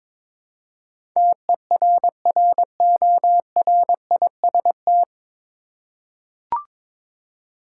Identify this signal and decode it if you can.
Morse code, heard as a keyed tone, about 22 wpm, 700 Hz. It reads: TERRORIST
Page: Morse Code Ninja practice 22wpm